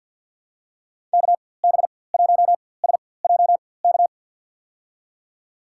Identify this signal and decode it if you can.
Morse code, heard as a keyed tone, about 53 wpm, spaced as for 24 wpm, 700 Hz. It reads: KB1SJK
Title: USA Call Signs - 6 Characters - 4x Character Spacing - 50wpm